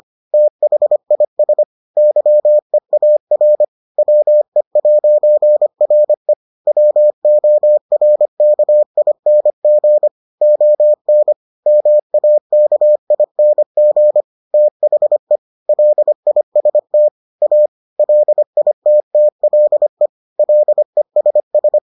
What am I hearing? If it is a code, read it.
Morse code, heard as a keyed tone, about 25 wpm, 600 Hz. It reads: THIS YEAR WE'RE WORKING ON MAKING THE LIST A LITTLE LESS